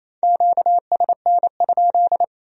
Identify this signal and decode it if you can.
Morse code, heard as a keyed tone, about 33 wpm, 700 Hz. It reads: QSN?
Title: Speed-Racing Q-Codes - 28wpm